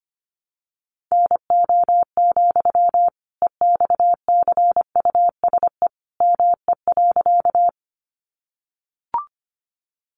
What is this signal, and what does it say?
Morse code, heard as a keyed tone, about 25 wpm, 700 Hz. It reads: NO, EXCUSE ME.
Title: Sentences from Top 700 Words 25wpm